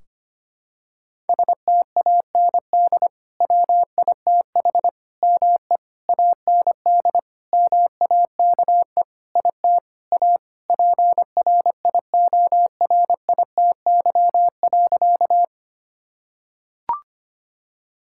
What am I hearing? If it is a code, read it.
Morse code, heard as a keyed tone, about 25 wpm, 700 Hz. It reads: STAND WITH ME AND MAKE IT A PRIORITY.
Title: Sentences from Top 1600 Words 25wpm